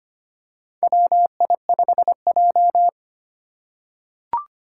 Morse code, heard as a keyed tone, about 25 wpm, 700 Hz. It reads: WI5J
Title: Four Challenging Letter-Number Combinations 25wpm